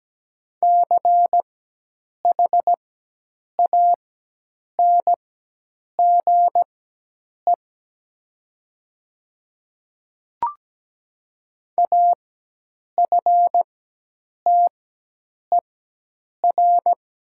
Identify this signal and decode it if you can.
Morse code, heard as a keyed tone, about 17 wpm, spaced as for 8 wpm, 700 Hz. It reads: CHANGE AFTER
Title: 4x Character Spacing - Top 300 Words - 17wpm